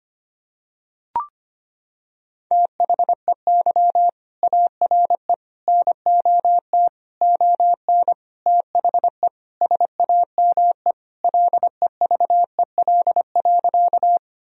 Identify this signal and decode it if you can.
Morse code, heard as a keyed tone, about 25 wpm, 700 Hz. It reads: THEY ARE NOT ON THE SAME LEVEL.
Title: Sentences from Top 1000 Words - Review Part 4 - 25wpm